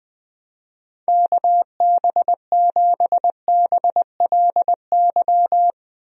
Morse code, heard as a keyed tone, about 21 wpm, 700 Hz. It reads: KB7BLY
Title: USA Call Signs - 6 Characters - Repeated 3x - 20wpm